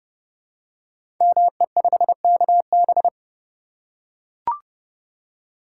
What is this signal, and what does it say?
Morse code, heard as a keyed tone, about 30 wpm, 700 Hz. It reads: ME5KB
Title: Code Groups - 5 Letter:Numbers - 30wpm